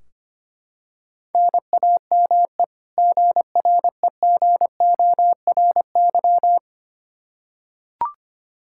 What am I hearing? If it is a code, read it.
Morse code, heard as a keyed tone, about 25 wpm, 700 Hz. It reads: NAME GREGORY
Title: Practice QSO - Exchange 1 - Beginning - 25wpm